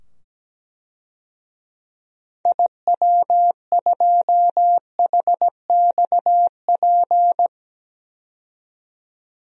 Morse code, heard as a keyed tone, about 17 wpm, 700 Hz. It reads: IW2HXP
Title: DXCC-100 Call Signs - 6 Characters - 17wpm